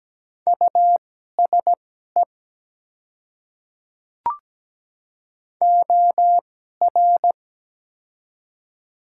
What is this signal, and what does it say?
Morse code, heard as a keyed tone, about 17 wpm, spaced as for 12 wpm, 690 Hz. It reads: USE OR
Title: 2x Character Spacing - Top 100 Words - 17wpm